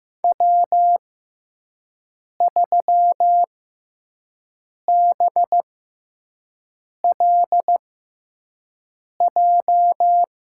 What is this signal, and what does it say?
Morse code, heard as a keyed tone, about 15 wpm, spaced as for 5 wpm, 700 Hz. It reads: W3BLJ
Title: USA Call Signs - 6x Character Spacing - 15wpm